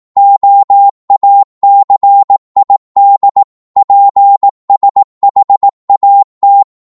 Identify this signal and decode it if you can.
Morse code, heard as a keyed tone, about 18 wpm, 800 Hz. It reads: OACID PSHAT
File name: Morse / ltr-18wpm